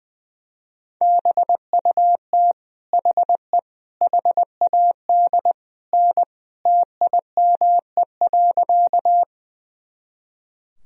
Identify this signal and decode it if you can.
Morse code, heard as a keyed tone, about 20 wpm, 700 Hz. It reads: BUT HE HAD N TIME.
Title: Sentences from Top 100 Words and Abbreviations 20wpm